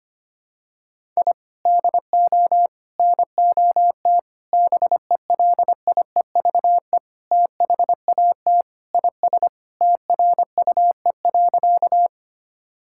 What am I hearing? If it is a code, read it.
Morse code, heard as a keyed tone, about 25 wpm, 700 Hz. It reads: I DO NOT BELIEVE THAT IS TRUE.
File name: practice / 25wpm / Sentences with 4-Word Phrases - 25wpm